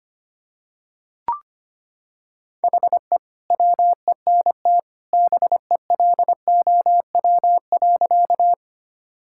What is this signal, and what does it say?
Morse code, heard as a keyed tone, about 25 wpm, 700 Hz. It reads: HE WENT BELOW.
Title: Sentences from Top 1000 Words - Review Part 5 - 25wpm